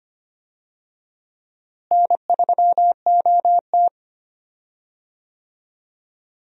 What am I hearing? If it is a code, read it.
Morse code, heard as a keyed tone, about 25 wpm, 700 Hz. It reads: N3OT
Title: USA Call Signs - 4 Characters - Repeated 3x - 25wpm